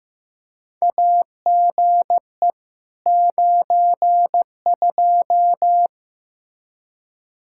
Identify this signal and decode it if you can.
Morse code, heard as a keyed tone, about 15 wpm, 700 Hz. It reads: AGE 92
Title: Practice QSO - Exchange 3 - More Info - 15wpm